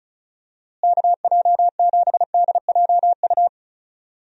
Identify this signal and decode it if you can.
Morse code, heard as a keyed tone, about 36 wpm, 700 Hz. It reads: KJ7DJU
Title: USA Call Signs - 6 Characters - Repeated 3x - 35wpm